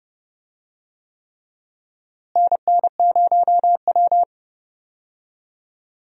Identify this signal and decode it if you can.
Morse code, heard as a keyed tone, about 29 wpm, 700 Hz. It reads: NN0W
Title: USA Call Signs - 4 Characters - Repeated 3x - 30wpm